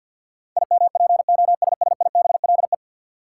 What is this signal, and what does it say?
Morse code, heard as a keyed tone, about 50 wpm, 700 Hz. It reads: IMPOSSIBLE
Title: Sentences from Top 1100 Words - New Word Drill - 50wpm